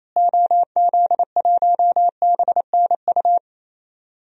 Morse code, heard as a keyed tone, about 27 wpm, 700 Hz. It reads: OZ1BNU
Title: DXCC-100 Call Signs - 6 Characters - Repeated 2x - 28wpm